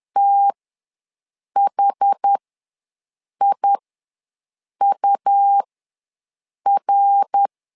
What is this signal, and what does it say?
Morse code, heard as a keyed tone, about 10 wpm, spaced as for 6 wpm, 800 Hz. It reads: THIUR
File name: LEZIONI / 3 MWRO / 3 MWR ws 10wpm 2151